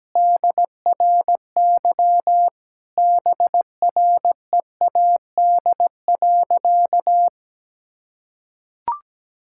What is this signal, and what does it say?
Morse code, heard as a keyed tone, about 17 wpm, 700 Hz. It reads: DRY BREAD.
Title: Sentences from Top 1600 Words 17wpm